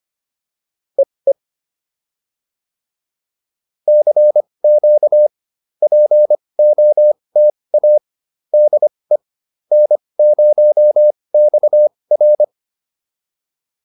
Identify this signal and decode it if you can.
Morse code, heard as a keyed tone, about 25 wpm, spaced as for 20 wpm, 600 Hz. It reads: EE CQ POTA DE N0XR